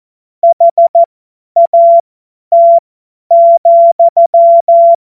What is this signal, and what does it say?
Morse code, heard as a keyed tone, about 14 wpm, spaced as for 10 wpm, 680 Hz. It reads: HAT,
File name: Morsekurs / Witze / Witz 11. Januar 2022, 50 F70